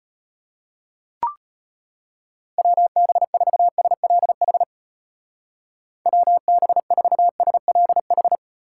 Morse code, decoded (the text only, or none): WB4SLH WB4SLH